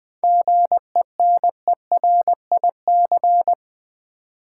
Morse code, heard as a keyed tone, about 20 wpm, 700 Hz. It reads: GENERIC
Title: 7 Letter Words - First 3 Letters Given - 20wpm